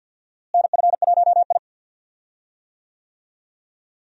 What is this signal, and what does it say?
Morse code, heard as a keyed tone, about 53 wpm, 700 Hz. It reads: NF1I
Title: USA Call Signs - 4 Characters - Repeated 2x - 50wpm